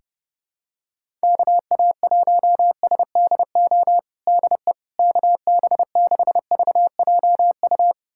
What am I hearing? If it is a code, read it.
Morse code, heard as a keyed tone, about 31 wpm, 700 Hz. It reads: KA1SDO DE KB6VJU
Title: USA Callsign DE Callsign - 6 Characters - 30wpm